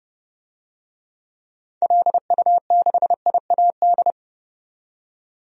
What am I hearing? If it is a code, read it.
Morse code, heard as a keyed tone, about 31 wpm, 700 Hz. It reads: LU6IAD
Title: DXCC-100 Call Signs - 6 Characters - 30wpm